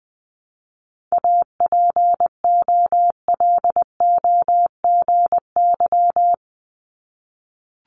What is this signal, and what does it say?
Morse code, heard as a keyed tone, about 20 wpm, 700 Hz. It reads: APOLOGY
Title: Multi-letter Phonemes Repeated 3x 20wpm